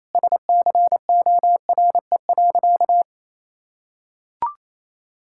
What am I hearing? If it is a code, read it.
Morse code, heard as a keyed tone, about 28 wpm, 700 Hz. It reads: SCORE.
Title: Sentences from Top 1600 Words 28wpm